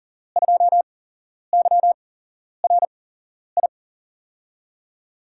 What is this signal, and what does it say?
Morse code, heard as a keyed tone, about 40 wpm, spaced as for 11 wpm, 700 Hz. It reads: 2YRI
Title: USA Call Signs - 6 Characters - 8x Character Spacing - 40wpm